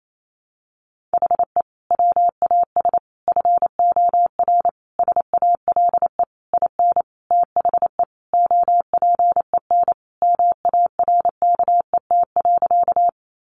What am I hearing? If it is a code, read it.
Morse code, heard as a keyed tone, about 28 wpm, 700 Hz. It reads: HE WAS FOR SALE IN THE OPEN MARKET.
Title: Sentences from Top 1400 Words 28wpm